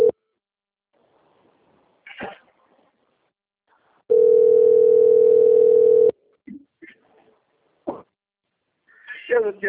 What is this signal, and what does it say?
Ringback tone (North America), 440 + 480 Hz, with speech.
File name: out- - - - - .25406